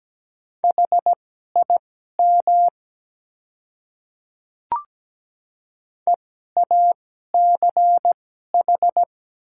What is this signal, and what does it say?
Morse code, heard as a keyed tone, about 17 wpm, spaced as for 12 wpm, 700 Hz. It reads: HIM EACH
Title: 2x Character Spacing - Top 100 Words - 17wpm